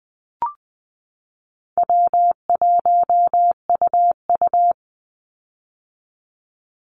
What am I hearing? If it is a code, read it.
Morse code, heard as a keyed tone, about 20 wpm, 700 Hz. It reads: W1UU
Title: Call Signs - 20wpm